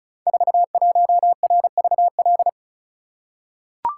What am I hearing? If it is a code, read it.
Morse code, heard as a keyed tone, about 36 wpm, 700 Hz. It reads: 41RVL
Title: Code Groups - 5 Letter:Numbers - 35wpm